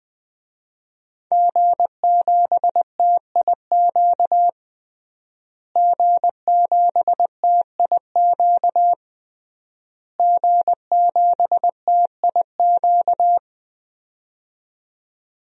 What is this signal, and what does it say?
Morse code, heard as a keyed tone, about 20 wpm, 700 Hz. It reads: G7TIQ G7TIQ G7TIQ
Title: DXCC-100 Call Signs - 5 Characters - Repeated 3x - 20wpm